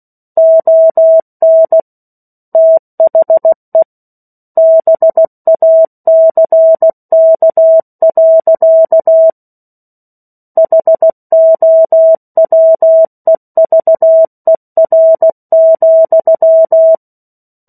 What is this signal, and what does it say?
Morse code, heard as a keyed tone, about 16 wpm, 650 Hz. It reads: ON THE BACK. HOWEVER,
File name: 03-Chapter-03 16 WPM